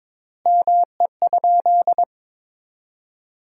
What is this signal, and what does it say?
Morse code, heard as a keyed tone, about 22 wpm, 700 Hz. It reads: ME?
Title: 3x Word-Spacing - Sentences from Top 300 Words - 22wpm